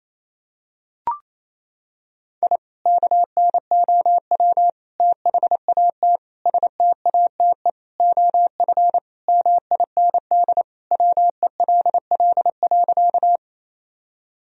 Morse code, heard as a keyed tone, about 28 wpm, 700 Hz. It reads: I KNOW THAT STATE OF MIND WELL.
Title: Sentences from Top 1000 Words - Review Part 2 - 28wpm